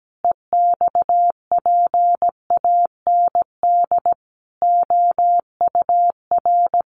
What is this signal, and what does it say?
Morse code, heard as a keyed tone, about 17 wpm, 700 Hz. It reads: EXPAND OUR
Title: Sentences from Top 1400 Words 17wpm